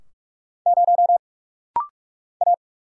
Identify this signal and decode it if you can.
Morse code, heard as a keyed tone, about 46 wpm, 700 Hz. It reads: 0 A
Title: Single Letter-Number - Rapid-Fire 45wpm